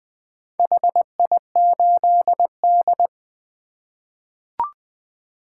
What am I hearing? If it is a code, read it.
Morse code, heard as a keyed tone, about 20 wpm, 700 Hz. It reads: HI8D
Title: 2023 POTA Activator Call Signs - 20wpm